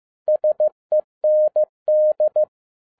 Morse code, heard as a keyed tone, about 15 wpm, 610 Hz. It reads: SEND
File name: 2012-01-mp3 / n13-A NEWLY DESIGNED QSL CARD-15wpm